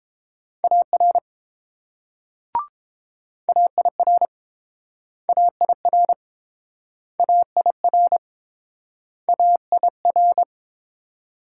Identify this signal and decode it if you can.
Morse code, heard as a keyed tone, about 29 wpm, 700 Hz. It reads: AR AIR AIR AIR AIR